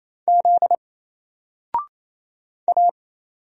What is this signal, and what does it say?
Morse code, heard as a keyed tone, about 28 wpm, 700 Hz. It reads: Z A
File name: Single-Letter-Number-28wpm